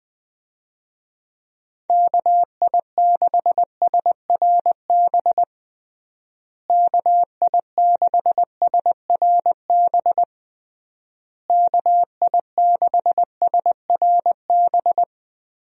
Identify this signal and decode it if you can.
Morse code, heard as a keyed tone, about 21 wpm, 700 Hz. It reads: KI6SRB KI6SRB KI6SRB
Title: USA Call Signs - 6 Characters - Repeated 3x - 20wpm